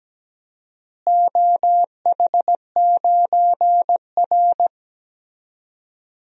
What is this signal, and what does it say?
Morse code, heard as a keyed tone, about 17 wpm, 700 Hz. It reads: OH9R
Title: DXCC-100 Call Signs - 4 Characters - Repeated 3x - 17wpm